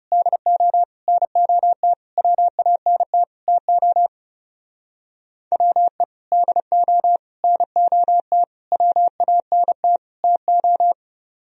Morse code, heard as a keyed tone, about 36 wpm, 700 Hz. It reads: DO NOT WANT TO WE DO NOT WANT TO
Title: Speed-Racing - Sets of 5 Words 30wpm